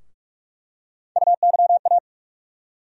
Morse code, heard as a keyed tone, about 43 wpm, 700 Hz. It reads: UYA
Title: USA Call Sign Postfixes - 3 Letters 45wpm